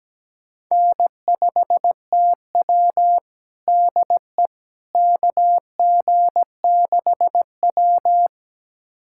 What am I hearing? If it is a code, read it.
Morse code, heard as a keyed tone, about 17 wpm, 700 Hz. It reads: N5TW DE KG6W
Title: USA Callsign DE Callsign - 4 Characters - 17wpm